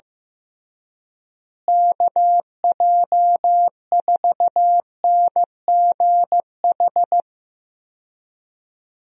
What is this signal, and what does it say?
Morse code, heard as a keyed tone, about 15 wpm, 700 Hz. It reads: KJ4NGH
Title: USA Call Signs - 6 Characters - Repeated 2x - 15wpm